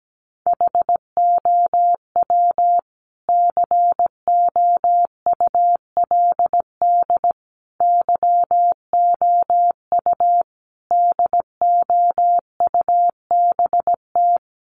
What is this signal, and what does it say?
Morse code, heard as a keyed tone, about 17 wpm, 700 Hz. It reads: HOW COULD YOU DOUBT
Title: Sentences from Top 1500 Words 17wpm